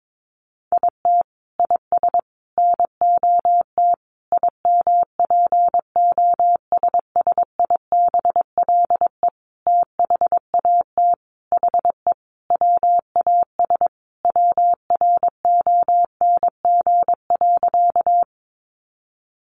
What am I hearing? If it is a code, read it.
Morse code, heard as a keyed tone, about 22 wpm, 700 Hz. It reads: IT IS NOT IMPOSSIBLE THAT HE WAS WRONG.
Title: Sentences from Top 1100 Words 22wpm